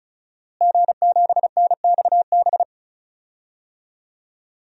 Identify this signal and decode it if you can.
Morse code, heard as a keyed tone, about 34 wpm, 700 Hz. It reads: G7NXB
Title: DXCC-100 Call Signs - 5 Characters - Repeated 2x - 35wpm